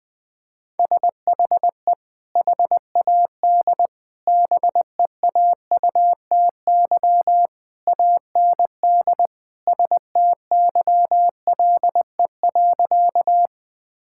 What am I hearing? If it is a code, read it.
Morse code, heard as a keyed tone, about 20 wpm, 700 Hz. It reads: SHE HAD BEAUTY AND STYLE.
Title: Sentences from Top 1000 Words - Review Part 5 - 20wpm